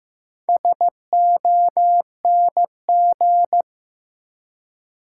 Morse code, heard as a keyed tone, about 15 wpm, 700 Hz. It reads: SONG
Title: Four Letter Words - 15wpm